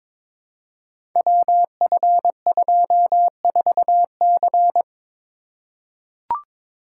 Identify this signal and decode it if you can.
Morse code, heard as a keyed tone, about 23 wpm, 700 Hz. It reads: WF24C
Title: Code Groups - 5 Letter:Numbers - 22wpm